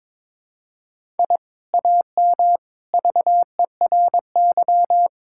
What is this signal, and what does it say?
Morse code, heard as a keyed tone, about 22 wpm, 700 Hz. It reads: I AM VERY
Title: Sentences from Top 1000 Words - Review Part 5 - 22wpm